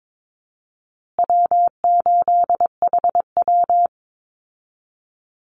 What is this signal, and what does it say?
Morse code, heard as a keyed tone, about 22 wpm, 700 Hz. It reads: W8HW
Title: Easy Call Signs Repeated 3x - 22wpm